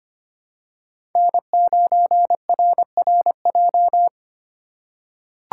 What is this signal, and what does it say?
Morse code, heard as a keyed tone, about 25 wpm, 700 Hz. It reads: N9RRJ
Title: USA Call Signs - 5 Characters - 4x Character Spacing - 25wpm